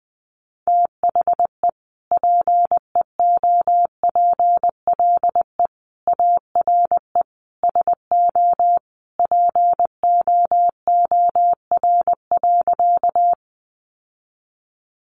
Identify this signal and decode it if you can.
Morse code, heard as a keyed tone, about 20 wpm, 700 Hz. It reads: THE PEOPLE ARE SO POOR.
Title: Sentences from Top 1000 Words 20wpm